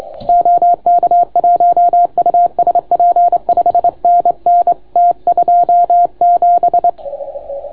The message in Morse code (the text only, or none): OK1USP 5NN T27